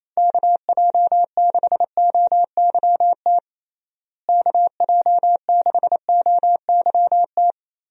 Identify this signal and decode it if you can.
Morse code, heard as a keyed tone, about 28 wpm, 700 Hz. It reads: KJ6OYT KJ6OYT
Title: USA Call Signs - 6 Characters - Repeated 2x - 28wpm